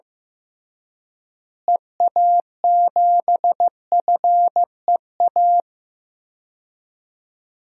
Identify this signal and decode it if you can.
Morse code, heard as a keyed tone, about 16 wpm, 700 Hz. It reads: EA7FEA
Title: DXCC-100 Call Signs - 6 Characters - Repeated 3x - 15wpm